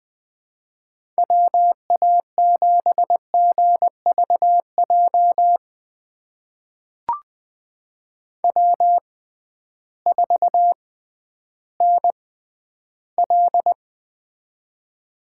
Morse code, heard as a keyed tone, about 20 wpm, spaced as for 12 wpm, 700 Hz. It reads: WA7GVJ W4NL
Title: USA Call Signs - 6x Character Spacing - 20wpm